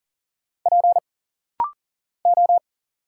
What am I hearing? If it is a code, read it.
Morse code, heard as a keyed tone, about 40 wpm, 700 Hz. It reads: P O
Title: Single Letter-Number Alt - Rapid-Fire 40wpm